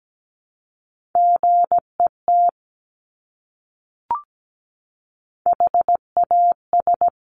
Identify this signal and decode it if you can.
Morse code, heard as a keyed tone, about 17 wpm, 700 Hz. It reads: GET HAS
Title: Top 200 Words Repeated 17wpm